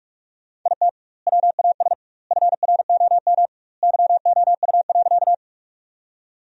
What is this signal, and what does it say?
Morse code, heard as a keyed tone, about 44 wpm, 700 Hz. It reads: IT WAS FROM YOU.
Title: 2x Word-Spacing - Sentences from Top 100 Words - 45wpm